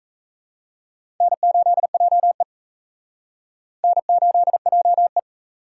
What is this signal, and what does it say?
Morse code, heard as a keyed tone, about 37 wpm, 700 Hz. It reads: N8JE N8JE